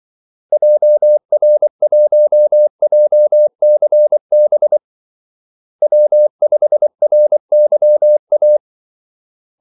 Morse code, heard as a keyed tone, about 24 wpm, 600 Hz. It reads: JR1JCB W5RYA
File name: Callsigns-24wpm